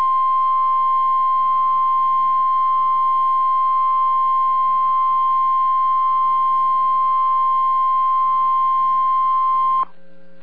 Emergency Alert System (SAME) tones (the 1050 Hz attention tone). It is heard from 0.5 to 9.5 s.